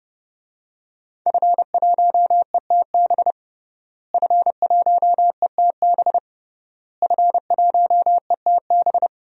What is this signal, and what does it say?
Morse code, heard as a keyed tone, about 32 wpm, 700 Hz. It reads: F1ETB F1ETB F1ETB
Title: DXCC-100 Call Signs - 5 Characters - Repeated 3x - 30wpm